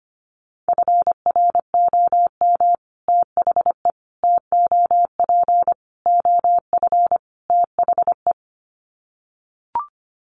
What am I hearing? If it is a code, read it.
Morse code, heard as a keyed tone, about 25 wpm, 700 Hz. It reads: FROM THE TOP OF THE
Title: Sets of 5 Words Encore 25wpm